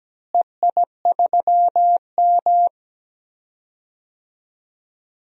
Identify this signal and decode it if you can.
Morse code, heard as a keyed tone, about 17 wpm, 700 Hz. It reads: EI3M
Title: DXCC-100 Call Signs - 4 Characters - Repeated 3x - 17wpm